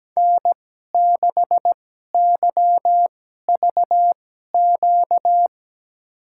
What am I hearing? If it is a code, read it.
Morse code, heard as a keyed tone, about 17 wpm, spaced as for 12 wpm, 700 Hz. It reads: N6YVQ
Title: USA Call Signs - 5 Characters - 2x Character Spacing - 17wpm